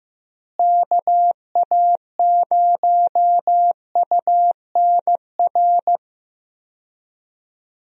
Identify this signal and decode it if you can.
Morse code, heard as a keyed tone, about 15 wpm, 700 Hz. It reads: KA0UNR
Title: USA Call Signs - 6 Characters - Repeated 2x - 15wpm